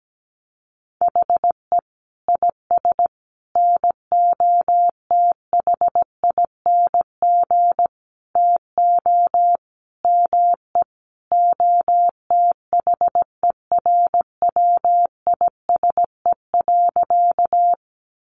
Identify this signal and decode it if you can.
Morse code, heard as a keyed tone, about 17 wpm, 700 Hz. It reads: HE IS NOTHING TO ME OTHERWISE.
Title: Sentences from Top 1200 Words 17wpm